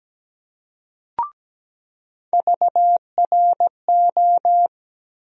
Morse code, heard as a keyed tone, about 17 wpm, 700 Hz. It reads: VRO
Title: USA Call Sign Postfixes - 3 Letters 17wpm